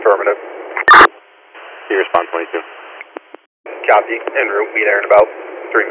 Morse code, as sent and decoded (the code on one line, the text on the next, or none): none